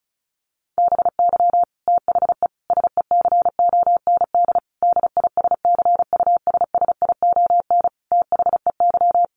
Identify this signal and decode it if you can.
Morse code, heard as a keyed tone, about 34 wpm, 700 Hz. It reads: BY THE SECOND DISCUSSION THEY
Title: Sentences from Top 1200 Words 35wpm